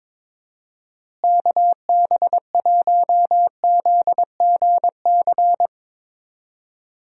Morse code, heard as a keyed tone, about 22 wpm, 700 Hz. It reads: KB1ZGC
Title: USA Call Signs - 6 Characters - Repeated 2x - 22wpm